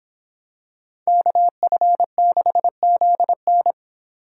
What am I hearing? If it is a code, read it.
Morse code, heard as a keyed tone, about 25 wpm, 700 Hz. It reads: KF6ZN